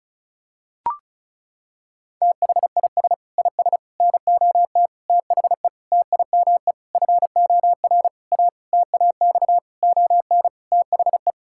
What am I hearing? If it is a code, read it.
Morse code, heard as a keyed tone, about 36 wpm, 700 Hz. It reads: THIS IS NOT THE TIME FOR A TAX ON THE
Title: Sentences from Top 1000 Words - Review Part 5 - 35wpm